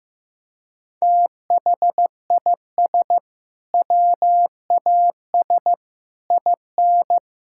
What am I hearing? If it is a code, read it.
Morse code, heard as a keyed tone, about 15 wpm, 700 Hz. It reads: THIS WAS IN
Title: Sentences from Top 1000 Words - Review Part 6 - 15wpm